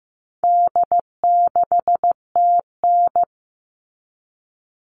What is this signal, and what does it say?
Morse code, heard as a keyed tone, about 15 wpm, 700 Hz. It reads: D6TN
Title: All Cut Numbers - 4-Digit 15wpm